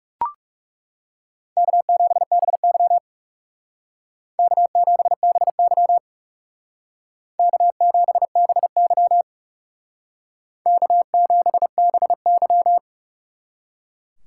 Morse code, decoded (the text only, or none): K7BY K7BY K7BY K7BY